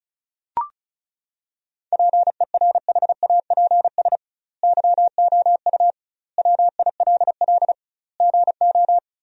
Morse code, heard as a keyed tone, about 36 wpm, 700 Hz. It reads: PERHAPS YOU WILL GO
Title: 2x Word-Spacing - Sentences from Top 500 Words - 35wpm